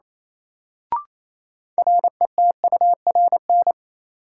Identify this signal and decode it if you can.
Morse code, heard as a keyed tone, about 27 wpm, 700 Hz. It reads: RETURN
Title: Sentences from Top 800 Words - New Word Drill - 28wpm